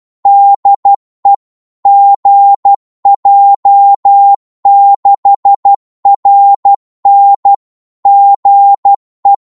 Morse code, heard as a keyed tone, about 12 wpm, 800 Hz. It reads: DE GJ6RN GE
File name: Morse / qso-12wpm